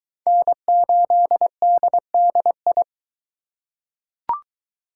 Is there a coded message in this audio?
Yes: Morse code from 0.3 to 2.8 s, heard as a keyed tone, about 21 wpm, 700 Hz. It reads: N8DDI